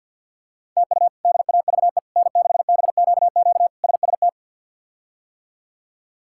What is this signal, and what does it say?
Morse code, heard as a keyed tone, about 50 wpm, 700 Hz. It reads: TU DAVE N6BQQ SST
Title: K1USN SST QSO Practice - 50wpm